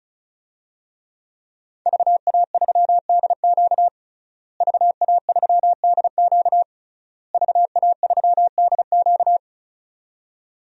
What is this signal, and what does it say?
Morse code, heard as a keyed tone, about 36 wpm, 700 Hz. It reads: VA3DQ VA3DQ VA3DQ
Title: DXCC-100 Call Signs - 5 Characters - Repeated 3x - 35wpm